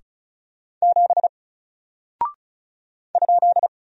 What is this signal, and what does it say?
Morse code, heard as a keyed tone, about 36 wpm, 700 Hz. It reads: 7 ?
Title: Single Character (Letter:Number:Punctuation:Prosign) 35wpm